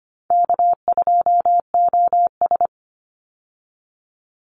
Morse code, heard as a keyed tone, about 25 wpm, 700 Hz. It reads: K2OS
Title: Call Signs Repeated 3x - 25wpm